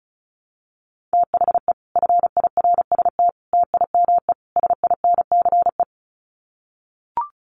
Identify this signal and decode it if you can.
Morse code, heard as a keyed tone, about 34 wpm, 700 Hz. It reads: THE FIRST TIME SINCE
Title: Sets of 4 Words 35wpm